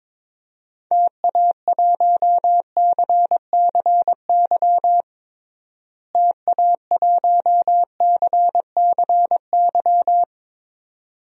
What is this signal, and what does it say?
Morse code, heard as a keyed tone, about 21 wpm, 700 Hz. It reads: TA1CCY TA1CCY
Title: DXCC-100 Call Signs - 6 Characters - Repeated 2x - 22wpm